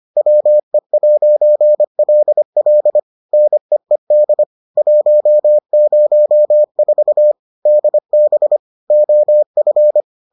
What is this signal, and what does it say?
Morse code, heard as a keyed tone, about 25 wpm, 600 Hz. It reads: WE'LL NEED 104 DB OF